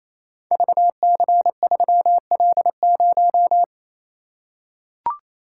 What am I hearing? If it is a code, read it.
Morse code, heard as a keyed tone, about 29 wpm, 700 Hz. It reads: VC3L0
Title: Code Groups - 5 Letter:Numbers - 28wpm